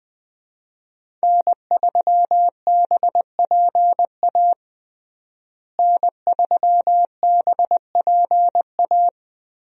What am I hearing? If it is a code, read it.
Morse code, heard as a keyed tone, about 20 wpm, 700 Hz. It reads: N3BPA N3BPA
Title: USA Call Signs - 5 Characters - Repeated 2x - 20wpm